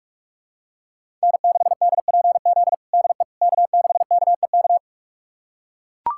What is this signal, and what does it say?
Morse code, heard as a keyed tone, about 44 wpm, 700 Hz. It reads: N6DPZ DE K6KEK
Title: USA Callsign DE Callsign - 5 Characters - 45wpm